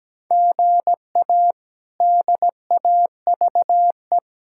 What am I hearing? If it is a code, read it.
Morse code, heard as a keyed tone, about 17 wpm, 700 Hz. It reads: GA DAVE
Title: K1USN SST QSO Practice - 17wpm